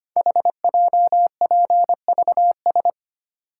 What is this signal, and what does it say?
Morse code, heard as a keyed tone, about 25 wpm, 700 Hz. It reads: HJPVS
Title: Code Groups - 5 Letters - 25wpm